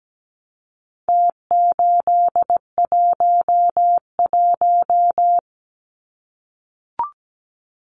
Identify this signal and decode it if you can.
Morse code, heard as a keyed tone, about 17 wpm, 700 Hz. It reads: T811
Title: Numbers - Contest Style - 4-Digit 17wpm